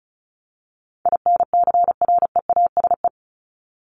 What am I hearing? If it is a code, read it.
Morse code, heard as a keyed tone, about 34 wpm, 700 Hz. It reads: INCREASE
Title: LY Suffix - Top 100 Words with Base Words 35wpm